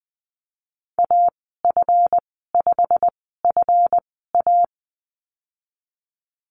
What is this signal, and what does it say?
Morse code, heard as a keyed tone, about 20 wpm, spaced as for 14 wpm, 700 Hz. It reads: AF5FA
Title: 140 Lesson 23g - Letter f - Call Signs 20wpm